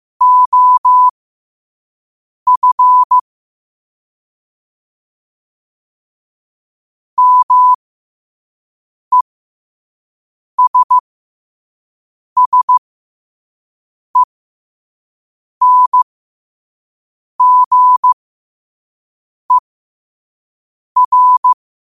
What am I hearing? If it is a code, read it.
Morse code, heard as a keyed tone, about 15 wpm, spaced as for 5 wpm, 1000 Hz. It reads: OF MESSENGER
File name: qotdfiles / 5wpm